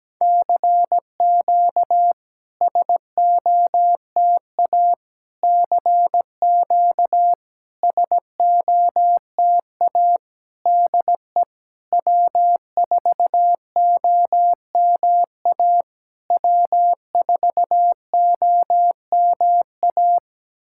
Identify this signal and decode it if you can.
Morse code, heard as a keyed tone, about 17 wpm, 700 Hz. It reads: CQ SOTA CQ SOTA DE W4OMA W4OMA
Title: SOTA QSO Practice 17wpm